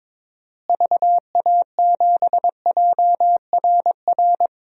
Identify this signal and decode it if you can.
Morse code, heard as a keyed tone, about 21 wpm, 700 Hz. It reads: VA7JRR
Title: DXCC-100 Call Signs - 6 Characters - Repeated 3x - 22wpm